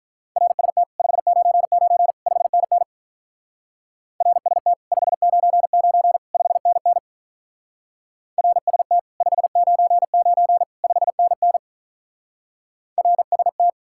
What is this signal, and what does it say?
Morse code, heard as a keyed tone, about 45 wpm, 700 Hz. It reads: RST 599 5NN RST 599 5NN RST 599 5NN RST